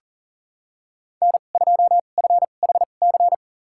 Morse code, heard as a keyed tone, about 40 wpm, spaced as for 29 wpm, 700 Hz. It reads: N2FHC
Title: USA Call Signs - 5 Characters - 2x Character Spacing - 40wpm